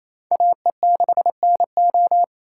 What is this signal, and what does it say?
Morse code, heard as a keyed tone, about 35 wpm, 700 Hz. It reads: AE6NO
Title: Speed Racing - USA 2x2 Call Signs 28wpm